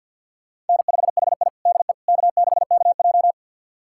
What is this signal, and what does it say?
Morse code, heard as a keyed tone, about 50 wpm, 700 Hz. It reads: N5HI DE K6KJ
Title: USA Callsign DE Callsign - 4 Characters - 50wpm